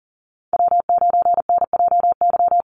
Morse code, heard as a keyed tone, about 40 wpm, 700 Hz. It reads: W9NJY
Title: Call Signs Repeated 3x - 40wpm